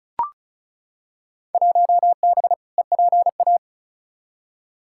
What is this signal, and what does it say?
Morse code, heard as a keyed tone, about 36 wpm, 700 Hz. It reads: 1B EPA
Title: Field Day Exchanges Encore - 35wpm